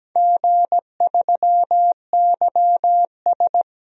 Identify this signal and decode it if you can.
Morse code, heard as a keyed tone, about 17 wpm, 700 Hz. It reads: G3YS
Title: DXCC-100 Call Signs - 4 Characters - Repeated 2x - 17wpm